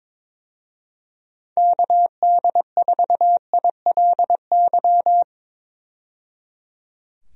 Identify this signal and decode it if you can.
Morse code, heard as a keyed tone, about 22 wpm, 700 Hz. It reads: KD4ILY
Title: USA Call Signs - 6 Characters - 22wpm